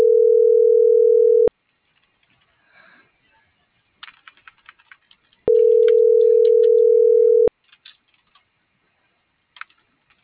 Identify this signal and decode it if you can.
Ringback tone (North America), 440 + 480 Hz.